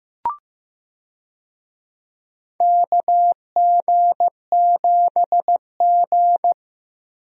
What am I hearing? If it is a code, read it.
Morse code, heard as a keyed tone, about 16 wpm, 700 Hz. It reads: KG7G
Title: USA Call Signs - 4 Characters - Repeated 2x - 15wpm